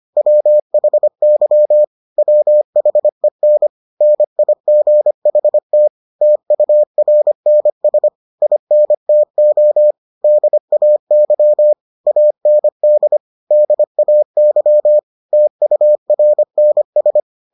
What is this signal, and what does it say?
Morse code, heard as a keyed tone, about 25 wpm, 600 Hz. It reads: WHY WHEN NIGHT TURNS INTO DAY AND DAY TURNS